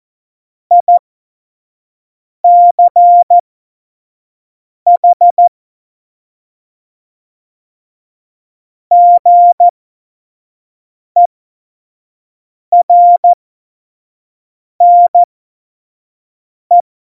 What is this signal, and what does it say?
Morse code, heard as a keyed tone, about 14 wpm, spaced as for 5 wpm, 700 Hz. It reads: ICH GERNE
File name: Morsekurs / Witze / Witz 29. August 17, Tempo 25 F70